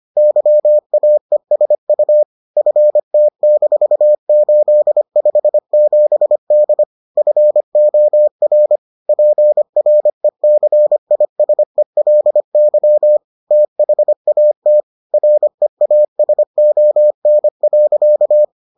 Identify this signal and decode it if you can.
Morse code, heard as a keyed tone, about 25 wpm, 600 Hz. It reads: YAESU FT-857D FOR PRECISELY THAT REASON.